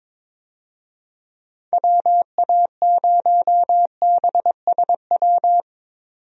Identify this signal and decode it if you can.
Morse code, heard as a keyed tone, about 22 wpm, 700 Hz. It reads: WA0BSW
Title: USA Call Signs - 6 Characters - 22wpm